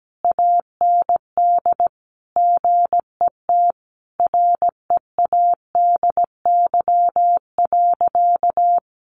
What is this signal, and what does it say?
Morse code, heard as a keyed tone, about 17 wpm, 700 Hz. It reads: AND GET READY.
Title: Sentences from Top 700 Words 17wpm